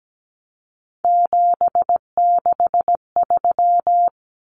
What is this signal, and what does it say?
Morse code, heard as a keyed tone, about 17 wpm, 700 Hz. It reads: 763
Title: Numbers - 3-Digit 17wpm